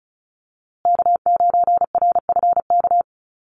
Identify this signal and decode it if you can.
Morse code, heard as a keyed tone, about 34 wpm, 700 Hz. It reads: K9RFK
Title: USA Call Signs Repeated 3x 35wpm